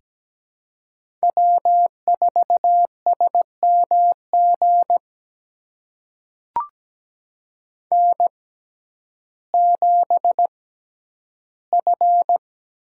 Morse code, heard as a keyed tone, about 17 wpm, spaced as for 10 wpm, 700 Hz. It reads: W4SMG N7F
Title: USA Call Signs - 5 Characters - 6x Character Spacing - 17wpm